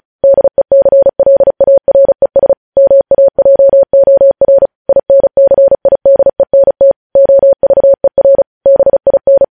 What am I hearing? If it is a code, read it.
Morse code, heard as a keyed tone, about 36 wpm, 550 Hz. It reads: DECLARES MAJOR INCIDENT OVER BIN
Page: News Headlines in Morse Code at 35 WPM